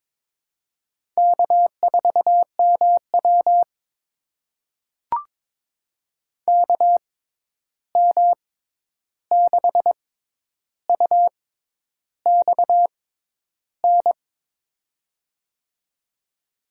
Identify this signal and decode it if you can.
Morse code, heard as a keyed tone, about 22 wpm, spaced as for 10 wpm, 700 Hz. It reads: K4MW KM6UXN
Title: USA Call Signs - 6x Character Spacing - 22wpm